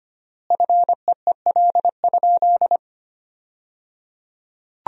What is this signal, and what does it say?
Morse code, heard as a keyed tone, about 25 wpm, 700 Hz. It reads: FEEL?
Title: 3x Word-Spacing - Sentences from Top 400 Words - 25wpm